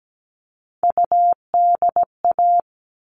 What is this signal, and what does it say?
Morse code, heard as a keyed tone, about 17 wpm, 700 Hz. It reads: UDA
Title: All Cut Numbers - 3-Digit 17wpm